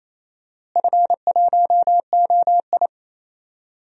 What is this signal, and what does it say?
Morse code, heard as a keyed tone, about 28 wpm, 700 Hz. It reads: F1OI
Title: DXCC-100 Call Signs - 4 Characters - Repeated 3x - 28wpm